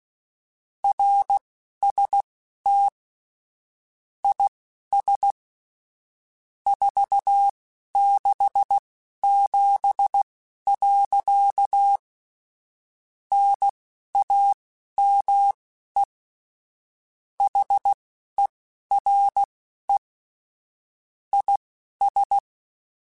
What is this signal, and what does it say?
Morse code, heard as a keyed tone, about 16 wpm, spaced as for 11 wpm, 800 Hz. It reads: RST IS 467. NAME HERE IS
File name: QSOs / QSOs-mp3s / WA7QOX 2@10